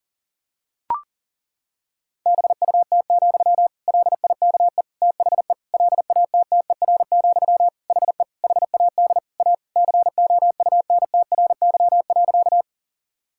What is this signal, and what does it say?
Morse code, heard as a keyed tone, about 40 wpm, 700 Hz. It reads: BUT, LIKE THE LATTER, HE HAD A COUNTRY.
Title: Sentences from Top 1600 Words 40wpm